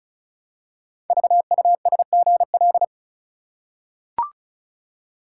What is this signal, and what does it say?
Morse code, heard as a keyed tone, about 36 wpm, 700 Hz. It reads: VUSGL
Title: Code Groups - 5 Letters - 35wpm